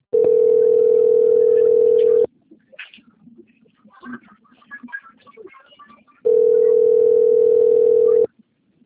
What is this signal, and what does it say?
Ringback tone (North America), 440 + 480 Hz.
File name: out- - - - - .22401